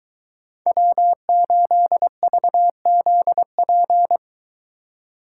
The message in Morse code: W8VZP

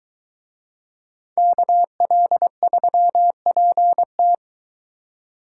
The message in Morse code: KL3PT

